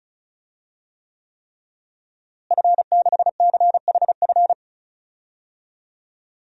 Morse code, heard as a keyed tone, about 34 wpm, 700 Hz. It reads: F6CHF
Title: DXCC-100 Call Signs - 5 Characters - Repeated 3x - 35wpm